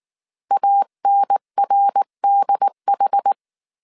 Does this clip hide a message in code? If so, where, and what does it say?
Yes: Morse code from 0.5 to 3.3 s, heard as a keyed tone, about 19 wpm, 800 Hz. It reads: ANRDH